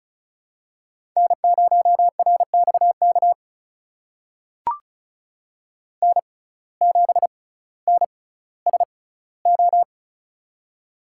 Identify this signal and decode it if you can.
Morse code, heard as a keyed tone, about 36 wpm, spaced as for 18 wpm, 700 Hz. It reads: N0RXK N7NSO
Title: USA Call Signs - 5 Characters - 6x Character Spacing - 35wpm